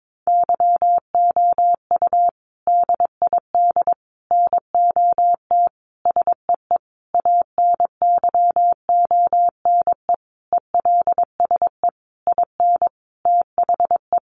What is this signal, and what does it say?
Morse code, heard as a keyed tone, about 22 wpm, 700 Hz. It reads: YOU DID NOT SEE ANYONE ELSE IN THE
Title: Sentences from Top 900 Words 22wpm